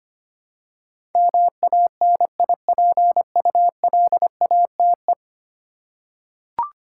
Morse code, heard as a keyed tone, about 25 wpm, 700 Hz. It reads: MANIPULATE
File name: Ten Letter Words - 25wpm